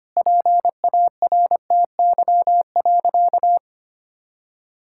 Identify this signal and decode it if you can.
Morse code, heard as a keyed tone, about 25 wpm, 700 Hz. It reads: PARTY.
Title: Sentences from Top 1600 Words 25wpm